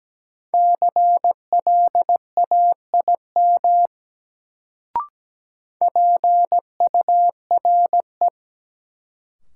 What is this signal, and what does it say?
Morse code, heard as a keyed tone, about 17 wpm, 700 Hz. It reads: CLAIM PURE
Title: Sentences from Top 1500 Words - New Word Drill - 17wpm